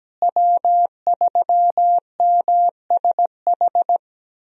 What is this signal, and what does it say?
Morse code, heard as a keyed tone, about 17 wpm, 700 Hz. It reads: W3MSH
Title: USA Call Signs - 5 Characters - Repeated 3x - 17wpm